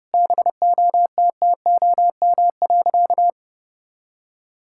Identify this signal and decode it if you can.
Morse code, heard as a keyed tone, about 32 wpm, 700 Hz. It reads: BOTTOM.
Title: Sentences from Top 1000 Words - Review Part 6 - 30wpm